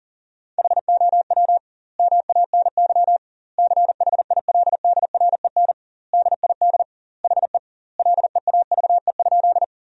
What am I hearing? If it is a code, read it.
Morse code, heard as a keyed tone, about 37 wpm, 700 Hz. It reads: HOW MANY CHILDREN DID HE LEAVE?
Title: 2x Word-Spacing - Sentences from Top 400 Words - 40wpm